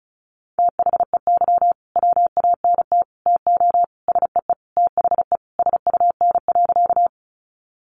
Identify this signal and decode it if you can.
Morse code, heard as a keyed tone, about 34 wpm, 700 Hz. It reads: THEY WANT TO SEE THE SUN.
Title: Easy Phrases 35wpm